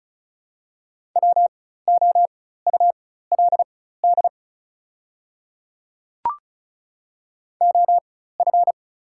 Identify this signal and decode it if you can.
Morse code, heard as a keyed tone, about 36 wpm, spaced as for 16 wpm, 690 Hz. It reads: WOULD OF
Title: 4x Character Spacing - Top 100 Words - 35wpm